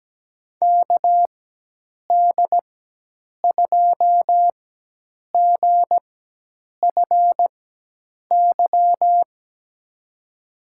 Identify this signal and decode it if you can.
Morse code, heard as a keyed tone, about 17 wpm, spaced as for 8 wpm, 700 Hz. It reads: KD2GFY
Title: USA Call Signs - 6 Characters - 4x Character Spacing - 17wpm